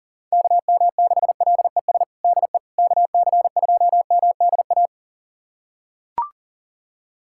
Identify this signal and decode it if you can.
Morse code, heard as a keyed tone, about 40 wpm, 700 Hz. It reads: KM6LES DE KC2MDA
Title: USA Callsign DE Callsign - 6 Characters - 40wpm